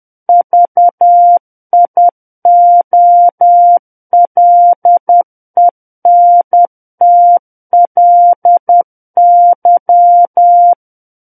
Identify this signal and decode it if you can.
Morse code, heard as a keyed tone, about 10 wpm, 700 Hz. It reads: VIOLENTLY
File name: Book One / 01-Chapter 1 - 10 WPM